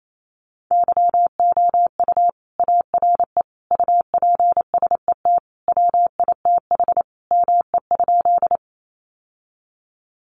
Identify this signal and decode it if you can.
Morse code, heard as a keyed tone, about 28 wpm, 700 Hz. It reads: YOU ARE UPSET WITH ME?
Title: Sentences from Top 800 Words 28wpm